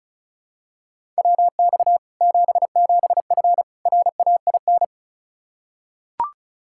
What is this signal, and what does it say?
Morse code, heard as a keyed tone, about 36 wpm, 700 Hz. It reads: WX 77F RAIN
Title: Practice QSO - Exchange 2 - Info - 35wpm